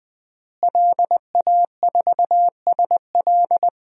Morse code, heard as a keyed tone, about 21 wpm, 700 Hz. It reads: LA4SL
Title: DXCC-100 Call Signs - 5 Characters - Repeated 3x - 20wpm